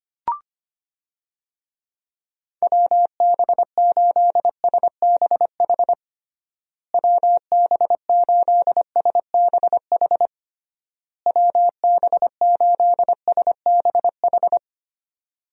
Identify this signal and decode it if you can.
Morse code, heard as a keyed tone, about 25 wpm, 700 Hz. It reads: WB8SBH WB8SBH WB8SBH
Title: USA Call Signs - 6 Characters - Repeated 3x - 25wpm